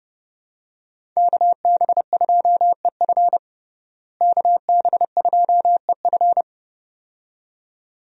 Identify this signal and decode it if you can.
Morse code, heard as a keyed tone, about 32 wpm, 700 Hz. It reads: KB2EF KB2EF
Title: USA Call Signs - 5 Characters - Repeated 2x - 30wpm